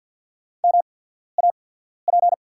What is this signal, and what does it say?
Morse code, heard as a keyed tone, about 48 wpm, spaced as for 14 wpm, 690 Hz. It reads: MAP
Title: 8x Character Spacing - Top 500 Words - 50wpm